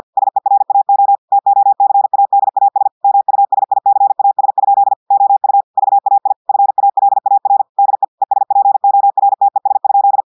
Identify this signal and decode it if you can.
Morse code, heard as a keyed tone, about 50 wpm, 800 Hz. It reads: SERAO TOCADAS MUSICAS? OU VAI FALAR DE ESPORTES?
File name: portugais texte en clair 2 vitesse 50 mots minutes